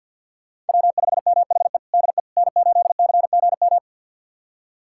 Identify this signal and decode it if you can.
Morse code, heard as a keyed tone, about 51 wpm, 700 Hz. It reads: W5MHE DE N8CGM
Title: USA Callsign DE Callsign - 5 Characters - 50wpm